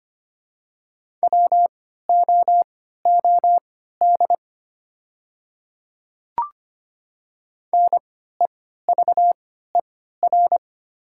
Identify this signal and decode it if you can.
Morse code, heard as a keyed tone, about 25 wpm, spaced as for 14 wpm, 700 Hz. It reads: WOOD NEVER
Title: 3x Character Spacing - Top 500 Words - 25wpm